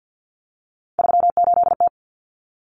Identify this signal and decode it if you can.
Morse code, heard as a keyed tone, about 50 wpm, 700 Hz. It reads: 38T
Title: Common Cut Numbers - 3-Digit 50wpm